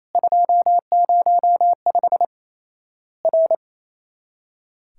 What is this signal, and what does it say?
Morse code, heard as a keyed tone, about 28 wpm, 700 Hz. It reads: 205 R
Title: Worked All Europe DX Contest - QTC Exchange - 28wpm